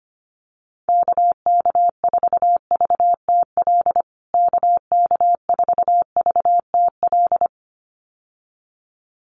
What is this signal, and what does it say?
Morse code, heard as a keyed tone, about 25 wpm, 700 Hz. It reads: KK4VTL KK4VTL
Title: USA Call Signs Repeated 2x 25wpm